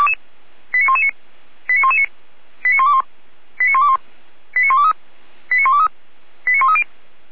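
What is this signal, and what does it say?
Five-tone selective call (ZVEI), heard as quick runs of five tones, tones of about 70 ms. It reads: 89119 89119 89121 89121 89123 89123 89130
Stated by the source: Five-tone ZVEI selective call, code 89010, received by SDR, from feuerwehr-orlamuende (Orlamünde fire brigade).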